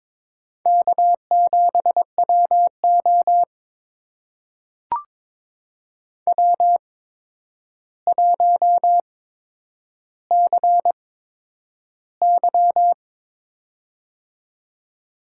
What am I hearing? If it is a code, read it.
Morse code, heard as a keyed tone, about 22 wpm, spaced as for 9 wpm, 700 Hz. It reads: K7WO W1CY
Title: USA Call Signs - 4 Characters - 8x Character Spacing - 22wpm